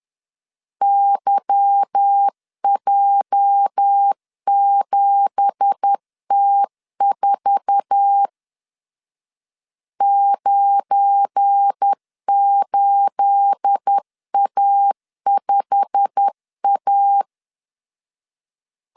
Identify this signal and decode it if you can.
Morse code, heard as a keyed tone, about 10 wpm, 800 Hz. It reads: YJ7T4 98A5A
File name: CW / DET / 1 9 1500C S B 10wpm 4439